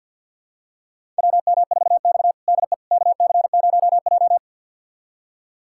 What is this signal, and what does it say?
Morse code, heard as a keyed tone, about 50 wpm, 700 Hz. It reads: WM4X DE KC0J
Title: USA Callsign DE Callsign - 4 Characters - 50wpm